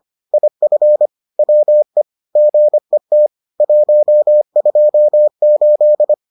Morse code, heard as a keyed tone, about 25 wpm, 600 Hz. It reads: IF WE GET 128